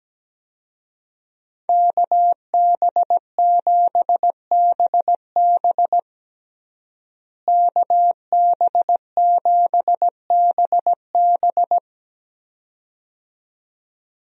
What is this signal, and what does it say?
Morse code, heard as a keyed tone, about 17 wpm, 700 Hz. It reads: KB7BB KB7BB
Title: USA Call Signs - 5 Characters - Repeated 2x - 17wpm